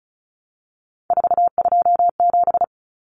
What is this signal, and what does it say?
Morse code, heard as a keyed tone, about 34 wpm, 700 Hz. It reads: 427
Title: Numbers - 3-Digit 35wpm